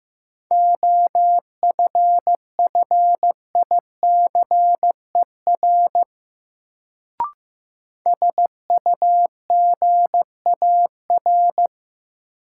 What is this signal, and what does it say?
Morse code, heard as a keyed tone, about 16 wpm, 700 Hz. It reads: OFFICER SUGAR
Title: Sentences from Top 1200 Words - New Word Drill - 15wpm